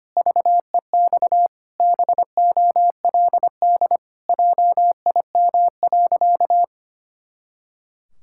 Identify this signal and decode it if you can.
Morse code, heard as a keyed tone, about 25 wpm, 700 Hz. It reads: VEX BOLD JIM.
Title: Pangram Challenge v2 - 25wpm